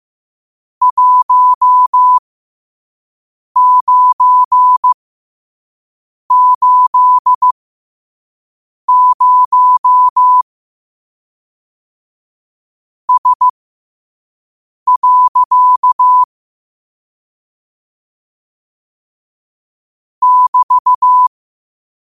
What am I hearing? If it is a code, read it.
Morse code, heard as a keyed tone, about 15 wpm, spaced as for 5 wpm, 1000 Hz. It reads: 1980S. <BT>
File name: qotdfiles / 5wpm